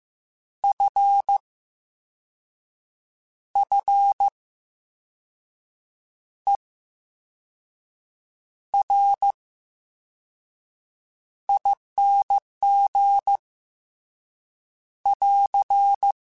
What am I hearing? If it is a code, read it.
Morse code, heard as a keyed tone, about 15 wpm, spaced as for 5 wpm, 800 Hz. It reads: FFERING <AR>
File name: Special INGs-mp3s / Special INGs-mp3s / VerySlowToFaster Ing @13